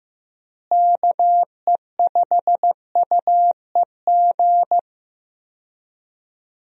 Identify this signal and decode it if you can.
Morse code, heard as a keyed tone, about 15 wpm, 700 Hz. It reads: KE5UEG
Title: USA Call Signs - 6 Characters - Repeated 2x - 15wpm